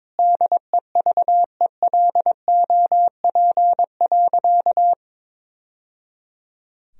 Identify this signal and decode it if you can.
Morse code, heard as a keyed tone, about 21 wpm, 700 Hz. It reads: DEVELOP.
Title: Sentences from Top 1000 Words - Review Part 2 - 22wpm